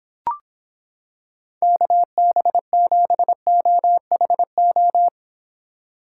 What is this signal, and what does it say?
Morse code, heard as a keyed tone, about 26 wpm, 700 Hz. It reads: KB7OHO